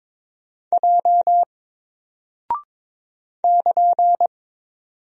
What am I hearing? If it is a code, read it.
Morse code, heard as a keyed tone, about 21 wpm, 700 Hz. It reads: J <KN>
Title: Single Character (Letter:Number:Punctuation:Prosign) Alt 22wpm